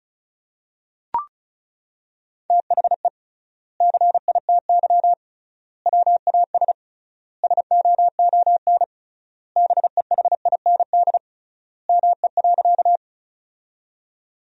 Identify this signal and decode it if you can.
Morse code, heard as a keyed tone, about 36 wpm, 700 Hz. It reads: THE CITY WAS SOON BEHIND ME.
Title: 3x Word-Spacing - Sentences from Top 500 Words - 35wpm